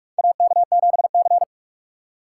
Morse code, heard as a keyed tone, about 42 wpm, 700 Hz. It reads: AK7C
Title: USA Call Signs - 4 Characters - 2x Character Spacing - 45wpm